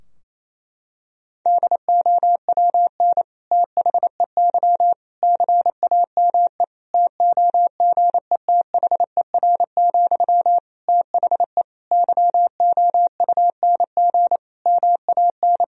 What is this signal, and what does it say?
Morse code, heard as a keyed tone, about 28 wpm, 700 Hz. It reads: DOWN THEY CAME TOGETHER, THE YOUNG MAN
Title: Sentences from Top 1000 Words - Review Part 2 - 28wpm